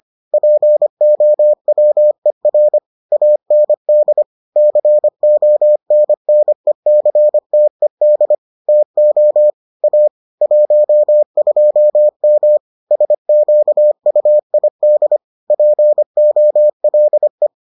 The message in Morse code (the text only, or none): POWER AND CONNECTED TO A 12M SQUID POLE